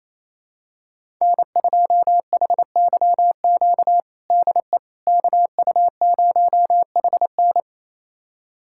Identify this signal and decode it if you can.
Morse code, heard as a keyed tone, about 28 wpm, 700 Hz. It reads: N2HYQ DE KU0HN
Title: USA Callsign DE Callsign - 5 Characters - 28wpm